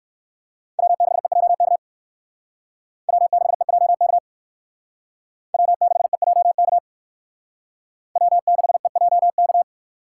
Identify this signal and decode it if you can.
Morse code, heard as a keyed tone, about 57 wpm, 700 Hz. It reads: W6EJK W6EJK W6EJK W6EJK